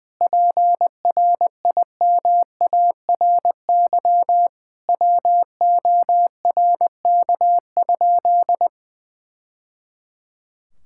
Morse code, heard as a keyed tone, about 20 wpm, 700 Hz. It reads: PRIMARY WORK?
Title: Sentences from Top 1000 Words - Review Part 5 - 20wpm